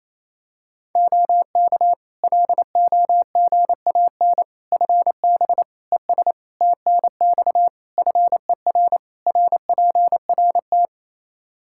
Morse code, heard as a keyed tone, about 28 wpm, 700 Hz. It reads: OK LOGAN FB ES TNX FER RPRT
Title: Practice QSO - Exchange 2 - Info - 28wpm